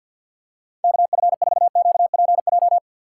Morse code, heard as a keyed tone, about 47 wpm, 700 Hz. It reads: KF4QPJ
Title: USA Call Signs - 4x Character Spacing - 50wpm